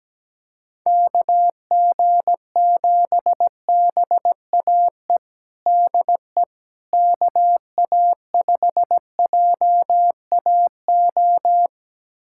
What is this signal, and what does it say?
Morse code, heard as a keyed tone, about 17 wpm, 700 Hz. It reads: KG7BAE DE KA5JAO
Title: USA Callsign DE Callsign - 6 Characters - 17wpm